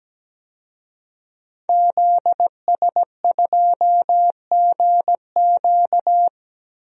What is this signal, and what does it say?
Morse code, heard as a keyed tone, about 17 wpm, 700 Hz. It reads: ZS2GQ
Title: DXCC-100 Call Signs - 5 Characters - 17wpm